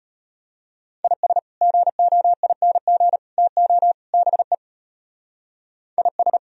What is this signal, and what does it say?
Morse code, heard as a keyed tone, about 30 wpm, 700 Hz. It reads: IS GOING TO BE IS
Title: Speed-Racing - Sets of 4 Words 25wpm